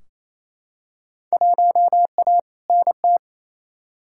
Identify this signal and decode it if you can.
Morse code, heard as a keyed tone, about 28 wpm, 700 Hz. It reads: 1A NT
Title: Field Day Exchanges Encore - 28wpm